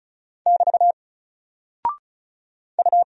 Morse code, heard as a keyed tone, about 36 wpm, 700 Hz. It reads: <BT> U
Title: Single Character (Letter:Number:Punctuation:Prosign) 35wpm